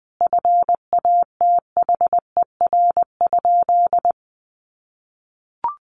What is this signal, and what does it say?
Morse code, heard as a keyed tone, about 20 wpm, 700 Hz. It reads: FATHER?
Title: Sentences from Top 400 Words 20wpm